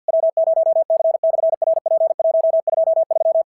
Morse code, heard as a keyed tone, about 46 wpm, 660 Hz. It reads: W0C/RP123
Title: SOTA QSO Practice 50wpm